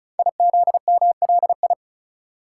Morse code, heard as a keyed tone, about 36 wpm, 690 Hz. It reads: IZMLI
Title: Code Groups - 5 Letters - 35wpm